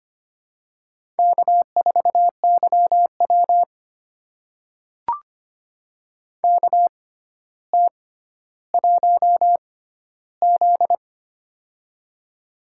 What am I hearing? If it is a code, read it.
Morse code, heard as a keyed tone, about 25 wpm, spaced as for 13 wpm, 700 Hz. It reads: K4YW KT1Z
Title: USA Call Signs - 4 Characters - 6x Character Spacing - 25wpm